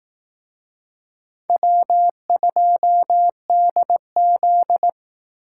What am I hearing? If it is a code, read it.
Morse code, heard as a keyed tone, about 18 wpm, 700 Hz. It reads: W2DZ
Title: Speed Racing - USA 1x2 Call Signs 15wpm